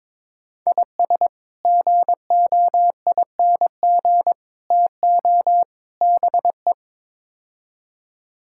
Morse code, heard as a keyed tone, about 22 wpm, 700 Hz. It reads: IS GOING TO BE
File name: practice / 22wpm / Build A Phrase - 5-Word Phrases - 22wpm